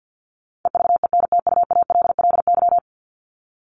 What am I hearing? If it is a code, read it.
Morse code, heard as a keyed tone, about 50 wpm, 700 Hz. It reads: EVENTUALLY
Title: Top 751-1000 Words 50wpm